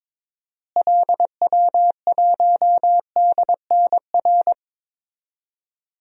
Morse code, heard as a keyed tone, about 22 wpm, 700 Hz. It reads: LW1DNR
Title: DXCC-100 Call Signs - 6 Characters - Repeated 2x - 22wpm